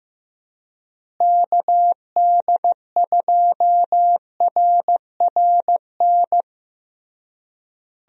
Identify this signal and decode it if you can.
Morse code, heard as a keyed tone, about 15 wpm, 700 Hz. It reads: KD2RRN
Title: USA Call Signs - 6 Characters - Repeated 3x - 15wpm